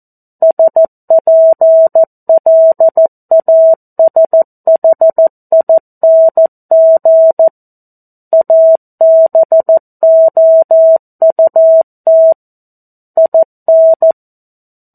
Morse code, heard as a keyed tone, about 14 wpm, 650 Hz. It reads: SPLASHING ABOUT IN
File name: 02-Chapter-02 14 WPM